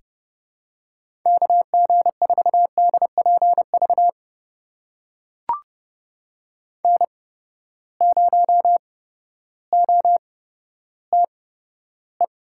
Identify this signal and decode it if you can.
Morse code, heard as a keyed tone, about 30 wpm, spaced as for 14 wpm, 700 Hz. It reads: KG4DPV N0OTE
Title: USA Call Signs - 8x Character Spacing - 30wpm